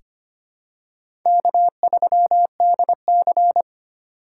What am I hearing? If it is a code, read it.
Morse code, heard as a keyed tone, about 25 wpm, 700 Hz. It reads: K3DC
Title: USA Call Signs - 4 Characters - 2x Character Spacing - 25wpm